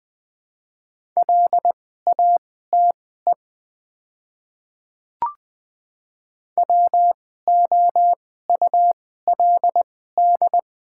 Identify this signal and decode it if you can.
Morse code, heard as a keyed tone, about 20 wpm, spaced as for 14 wpm, 690 Hz. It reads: LATE WOULD
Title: 2x Character Spacing - Top 300 Words - 20wpm